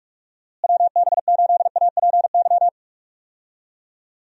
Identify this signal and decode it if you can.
Morse code, heard as a keyed tone, about 41 wpm, 700 Hz. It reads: WB8APY
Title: USA Call Signs - 6 Characters - Repeated 2x - 45wpm